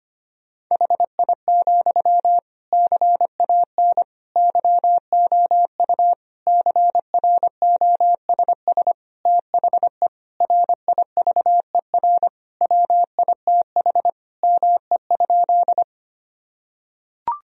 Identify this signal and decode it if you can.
Morse code, heard as a keyed tone, about 25 wpm, 700 Hz. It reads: HI, CAN YOU CROSS THE RIVER WITH ME?
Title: Sentences from Top 1000 Words - Review Part 3 - 25wpm